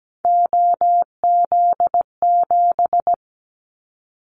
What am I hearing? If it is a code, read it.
Morse code, heard as a keyed tone, about 17 wpm, 700 Hz. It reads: OZ7
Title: Callsign Prefixes for Top 100 DXCC - 17wpm